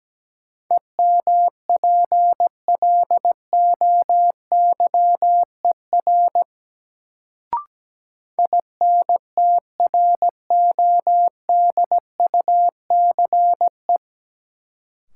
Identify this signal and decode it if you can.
Morse code, heard as a keyed tone, about 17 wpm, 700 Hz. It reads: EMPLOYER INTRODUCE
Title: Sentences from Top 1600 Words - New Word Drill - 17wpm